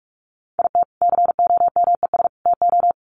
Morse code, heard as a keyed tone, about 44 wpm, 700 Hz. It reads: IT COMES TO
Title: Sets of 3 Words 45wpm